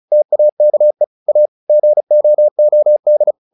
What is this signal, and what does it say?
Morse code, heard as a keyed tone, about 34 wpm, 600 Hz. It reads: TAKE A GOOD
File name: the old man and the sea / the old man and the sea 1 35wpm